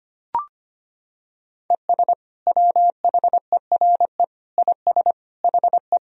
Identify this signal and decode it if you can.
Morse code, heard as a keyed tone, about 25 wpm, 700 Hz. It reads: ES WHERE IS HE
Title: Sentences from Top 200 Words and Abbreviations 25wpm